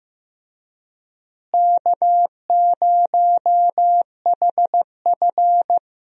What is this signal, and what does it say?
Morse code, heard as a keyed tone, about 15 wpm, 700 Hz. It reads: K0HF
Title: USA Call Signs - 4 Characters - Repeated 2x - 15wpm